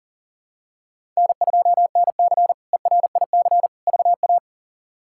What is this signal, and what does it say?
Morse code, heard as a keyed tone, about 40 wpm, 700 Hz. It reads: N2NC ERIC VA
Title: K1USN SST QSO Practice - 40wpm